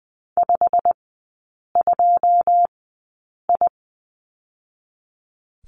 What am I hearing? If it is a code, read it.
Morse code, heard as a keyed tone, about 20 wpm, 700 Hz. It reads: 5 2 I
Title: 114 Lesson 19e - Letter w - Three Characters 20wpm